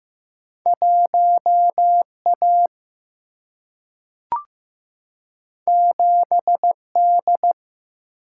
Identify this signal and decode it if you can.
Morse code, heard as a keyed tone, about 15 wpm, 700 Hz. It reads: 1 A 7 D
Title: Single-Letter Single-Number 15wpm